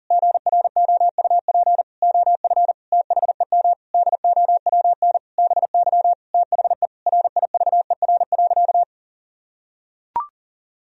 Morse code, heard as a keyed tone, about 37 wpm, 700 Hz. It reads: GROUP OF THEM DOWN BY THE RIVER.
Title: Sentences from Top 1000 Words - Review Part 1 - 40wpm